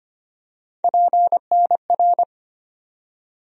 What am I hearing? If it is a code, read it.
Morse code, heard as a keyed tone, about 25 wpm, 700 Hz. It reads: PNR
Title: USA Call Sign Postfixes - 3 Letters 25wpm